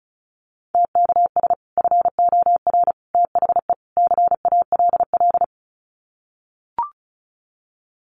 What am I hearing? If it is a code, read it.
Morse code, heard as a keyed tone, about 34 wpm, 700 Hz. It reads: TKS FOR THE CALL
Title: Common QSO Phrases 35wpm